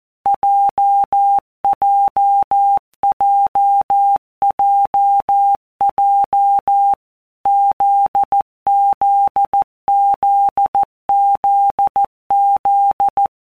Morse code, heard as a keyed tone, about 14 wpm, 800 Hz. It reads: JJJJJ ZZZZZ
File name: Basic Letter Groups By WPM's / Groups@13 / JZ17@13